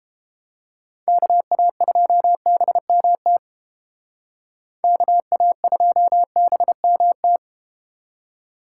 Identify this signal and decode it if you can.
Morse code, heard as a keyed tone, about 32 wpm, 700 Hz. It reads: KA2BMT KA2BMT